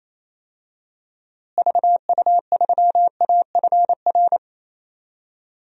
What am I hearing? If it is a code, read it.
Morse code, heard as a keyed tone, about 28 wpm, 700 Hz. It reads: VU3AFR
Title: DXCC-100 Call Signs - 6 Characters - Repeated 3x - 28wpm